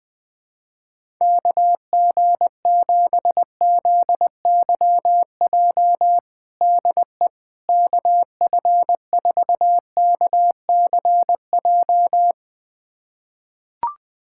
Morse code, heard as a keyed tone, about 21 wpm, 700 Hz. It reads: KG7ZYJ DE KF4KCJ
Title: USA Callsign DE Callsign - 6 Characters - 20wpm